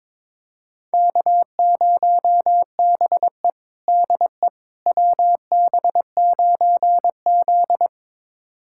Morse code, heard as a keyed tone, about 21 wpm, 700 Hz. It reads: K0BE DE WB9Z
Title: USA Callsign DE Callsign - 4 Characters - 22wpm